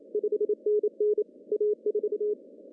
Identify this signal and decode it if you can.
Morse code, heard as a keyed tone, about 29 wpm, 420 Hz. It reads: 5NN A4
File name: CW / cqww-cw-k5tr